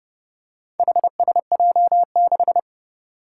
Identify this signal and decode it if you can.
Morse code, heard as a keyed tone, about 30 wpm, 700 Hz. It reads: HSJ6
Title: Four Challenging Letter-Number Combinations 30wpm